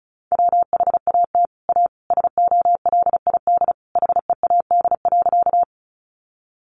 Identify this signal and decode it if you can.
Morse code, heard as a keyed tone, about 34 wpm, 700 Hz. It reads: WHAT A SOLID HEAD.
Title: Sentences from Top 1500 Words 35wpm